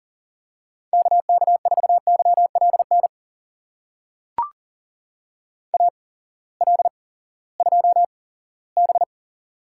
Morse code, heard as a keyed tone, about 40 wpm, spaced as for 20 wpm, 700 Hz. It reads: KK4YLN AL2B
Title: USA Call Signs - 8x Character Spacing - 40wpm